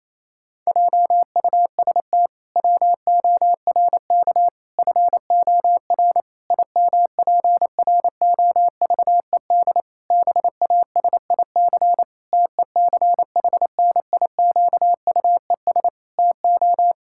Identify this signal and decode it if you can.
Morse code, heard as a keyed tone, about 28 wpm, 700 Hz. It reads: JUST WORK FOR IMPROVED BASIC TECHNIQUES TO
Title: Pangram Challenge v2 - 28wpm